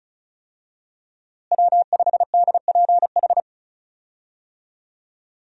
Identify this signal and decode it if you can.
Morse code, heard as a keyed tone, about 36 wpm, 700 Hz. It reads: W5DPH
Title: USA Call Signs - 5 Characters - Repeated 2x - 35wpm